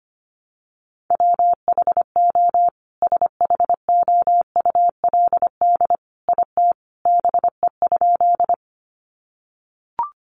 Morse code, heard as a keyed tone, about 25 wpm, 700 Hz. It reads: WHO SHOULD IT BE?
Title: Sentences from Top 300 Words 25wpm